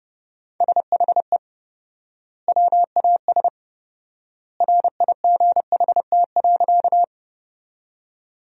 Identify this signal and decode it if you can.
Morse code, heard as a keyed tone, about 30 wpm, 700 Hz. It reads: SHE WAS RIGHT.
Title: 4x Word-Spacing - Sentences from Top 200 Words - 30wpm